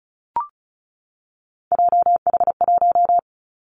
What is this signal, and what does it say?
Morse code, heard as a keyed tone, about 36 wpm, 700 Hz. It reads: JH1
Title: Callsign Prefixes for Top 25 DXCC - 35wpm